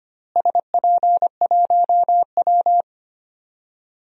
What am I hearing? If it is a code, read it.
Morse code, heard as a keyed tone, about 25 wpm, 700 Hz. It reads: SP1W
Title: DXCC-100 Call Signs - 4 Characters - Repeated 3x - 25wpm